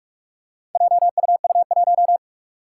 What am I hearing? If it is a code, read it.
Morse code, heard as a keyed tone, about 44 wpm, 700 Hz. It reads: JUU1
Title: Four Challenging Letter-Number Combinations 45wpm